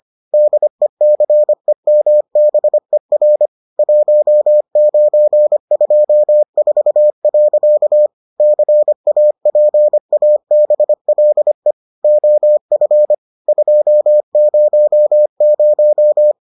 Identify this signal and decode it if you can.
Morse code, heard as a keyed tone, about 25 wpm, 600 Hz. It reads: DECEMBER 1924. CAPABLE OF 200